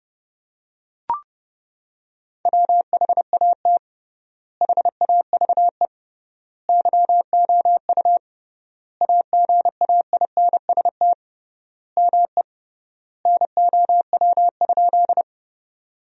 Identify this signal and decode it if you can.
Morse code, heard as a keyed tone, about 30 wpm, 700 Hz. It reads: WHAT HAVE YOU AGAINST ME NOW?
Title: 3x Word-Spacing - Sentences from Top 400 Words - 30wpm